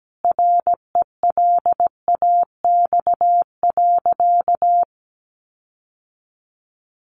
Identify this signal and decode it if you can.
Morse code, heard as a keyed tone, about 17 wpm, 700 Hz. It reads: RELAX.
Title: Sentences from Top 1400 Words 17wpm